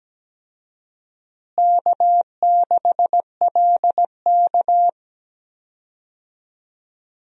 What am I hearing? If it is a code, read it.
Morse code, heard as a keyed tone, about 17 wpm, 700 Hz. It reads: K6LK
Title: USA Call Signs - 4 Characters - Repeated 3x - 17wpm